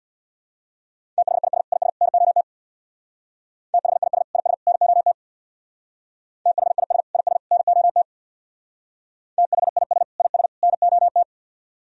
Morse code, heard as a keyed tone, about 62 wpm, 700 Hz. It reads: THIS IS NOT THIS IS NOT THIS IS NOT THIS IS NOT
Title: Speed-Racing - Sets of 3 Words 50wpm